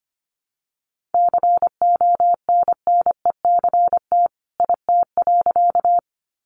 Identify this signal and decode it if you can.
Morse code, heard as a keyed tone, about 25 wpm, 700 Hz. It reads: CONNECT IT.
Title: Sentences from Top 1500 Words 25wpm